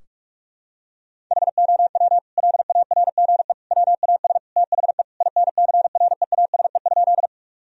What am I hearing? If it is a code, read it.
Morse code, heard as a keyed tone, about 41 wpm, 700 Hz. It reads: HOW LARGE WAS THE INCREASE?
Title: Sentences from Top 1000 Words - Review Part 4 - 45wpm